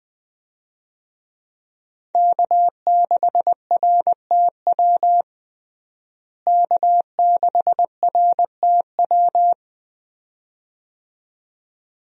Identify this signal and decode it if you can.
Morse code, heard as a keyed tone, about 20 wpm, 700 Hz. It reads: K6RTW K6RTW
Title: USA Call Signs - 5 Characters - Repeated 2x - 20wpm